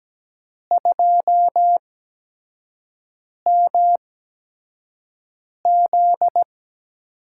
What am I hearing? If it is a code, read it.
Morse code, heard as a keyed tone, about 17 wpm, spaced as for 5 wpm, 700 Hz. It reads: 2MZ
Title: USA Call Signs - 5 Characters - 8x Character Spacing - 17wpm